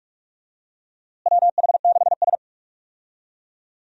Morse code, heard as a keyed tone, about 45 wpm, 700 Hz. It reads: WH6S
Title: USA Call Signs - 4 Characters - Repeated 2x - 45wpm